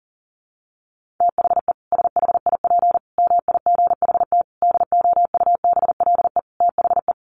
Morse code, heard as a keyed tone, about 40 wpm, 700 Hz. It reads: THE SHIP MIGHT DOUBLE THE
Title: Sentences from Top 1400 Words 40wpm